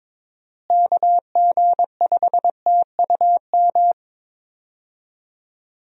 Morse code, heard as a keyed tone, about 22 wpm, 700 Hz. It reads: KG5TUM
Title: USA Call Signs - 6 Characters - Repeated 3x - 22wpm